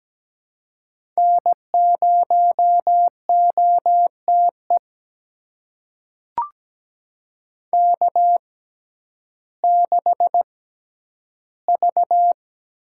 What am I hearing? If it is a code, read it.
Morse code, heard as a keyed tone, about 17 wpm, spaced as for 10 wpm, 700 Hz. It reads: N0OTE K6V
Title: USA Call Signs - 5 Characters - 6x Character Spacing - 17wpm